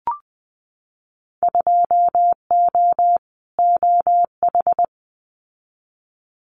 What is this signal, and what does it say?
Morse code, heard as a keyed tone, about 20 wpm, 700 Hz. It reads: 2O OH
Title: Winter Field Day Exchanges 20wpm